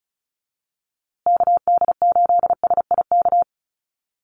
Morse code, heard as a keyed tone, about 34 wpm, 700 Hz. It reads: KD8SIK
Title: USA Call Signs Repeated 3x 35wpm